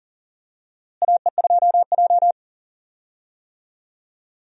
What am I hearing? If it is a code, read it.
Morse code, heard as a keyed tone, about 40 wpm, 700 Hz. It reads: AE2J
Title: USA Call Signs - 4 Characters - Repeated 3x - 40wpm